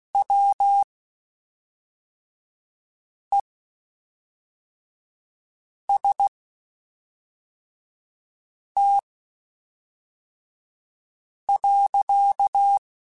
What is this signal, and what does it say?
Morse code, heard as a keyed tone, about 16 wpm, spaced as for 3 wpm, 800 Hz. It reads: WEST.
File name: Spaced Characters / Spaced Characters-mp3s / Weather5 1 sp.@5